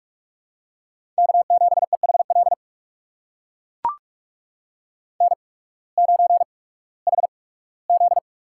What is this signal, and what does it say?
Morse code, heard as a keyed tone, about 44 wpm, spaced as for 21 wpm, 700 Hz. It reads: K7EHL N9HZ
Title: USA Call Signs - 5 Characters - 8x Character Spacing - 45wpm